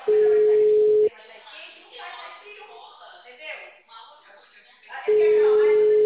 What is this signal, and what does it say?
Ringback tone (Europe, 425 Hz), with speech.